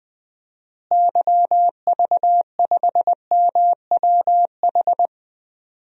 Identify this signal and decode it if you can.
Morse code, heard as a keyed tone, about 20 wpm, 700 Hz. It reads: YV5MWH
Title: DXCC-100 Call Signs - 6 Characters - Repeated 2x - 20wpm